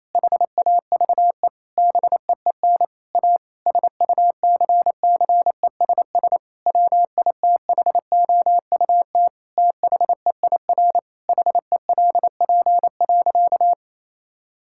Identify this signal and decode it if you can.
Morse code, heard as a keyed tone, about 28 wpm, 700 Hz. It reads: HAVE BEEN A SUCCESS WITHOUT THEIR HELP.
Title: Sentences from Top 1000 Words - Review Part 6 - 28wpm